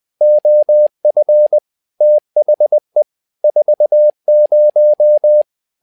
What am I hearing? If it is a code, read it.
Morse code, heard as a keyed tone, about 20 wpm, 600 Hz. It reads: OF THE 40
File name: tower trouble 20wpm